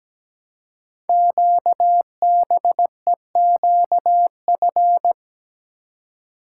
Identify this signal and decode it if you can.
Morse code, heard as a keyed tone, about 17 wpm, 700 Hz. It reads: QBEQF
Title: Code Groups - 5 Letters - 17wpm